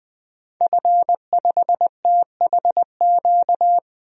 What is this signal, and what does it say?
Morse code, heard as a keyed tone, about 21 wpm, 700 Hz. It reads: F5THQ
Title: DXCC-100 Call Signs - 5 Characters - 20wpm